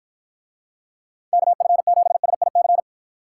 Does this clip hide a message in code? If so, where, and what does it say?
Yes: Morse code from 1.3 to 2.8 s, heard as a keyed tone, about 53 wpm, 700 Hz. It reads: KF7SIC